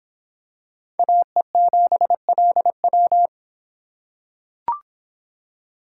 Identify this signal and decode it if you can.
Morse code, heard as a keyed tone, about 26 wpm, 700 Hz. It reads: AE7LW